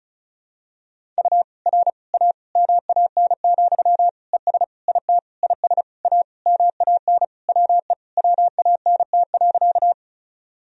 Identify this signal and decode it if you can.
Morse code, heard as a keyed tone, about 36 wpm, 700 Hz. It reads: U R A MAN, ES IT IS A MAN WE WANT.
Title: Sentences from Top 200 Words and Abbreviations 35wpm